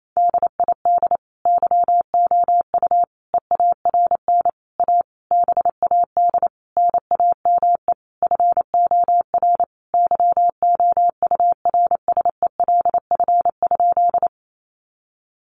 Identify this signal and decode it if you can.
Morse code, heard as a keyed tone, about 28 wpm, 700 Hz. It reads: DID YOU EARN A BAD NAME FOR YOURSELF?
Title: Sentences from Top 1500 Words 28wpm